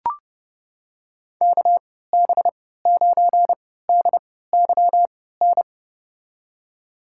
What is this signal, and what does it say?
Morse code, heard as a keyed tone, about 30 wpm, spaced as for 17 wpm, 700 Hz. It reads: KB9DYN
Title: USA Call Signs - 3x Character Spacing - 30wpm